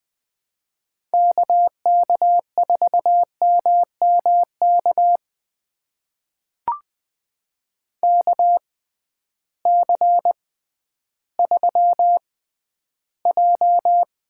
Morse code, heard as a keyed tone, about 20 wpm, spaced as for 12 wpm, 700 Hz. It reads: KK4MMK KC3J
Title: USA Call Signs - 6 Characters - 6x Character Spacing - 20wpm